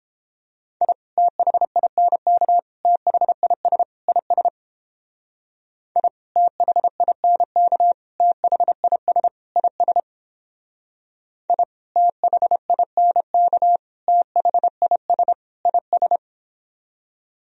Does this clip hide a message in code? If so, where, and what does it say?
Yes: Morse code from 0.8 to 16.2 s, heard as a keyed tone, about 29 wpm, 700 Hz. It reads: I THINK THIS IS I THINK THIS IS I THINK THIS IS